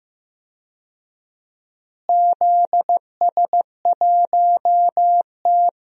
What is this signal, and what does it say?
Morse code, heard as a keyed tone, about 16 wpm, 700 Hz. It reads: ZS1T
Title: DXCC-100 Call Signs - 4 Characters - Repeated 3x - 15wpm